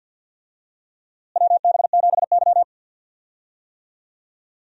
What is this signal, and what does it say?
Morse code, heard as a keyed tone, about 47 wpm, 700 Hz. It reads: WB7Y
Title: USA Call Signs - 4 Characters - Repeated 2x - 50wpm